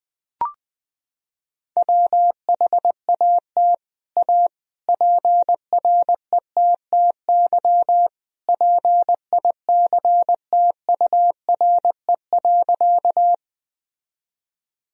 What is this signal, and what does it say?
Morse code, heard as a keyed tone, about 20 wpm, 700 Hz. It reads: WHAT A PRETTY PICTURE.
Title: Sentences from Top 1000 Words - Review Part 1 - 20wpm